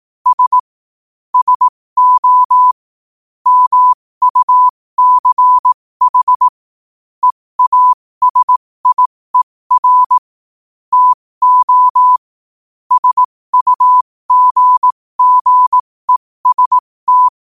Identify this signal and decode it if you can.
Morse code, heard as a keyed tone, about 18 wpm, spaced as for 15 wpm, 1000 Hz. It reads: S SO MUCH EASIER TO SUGGEST
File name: qotdfiles / 15wpm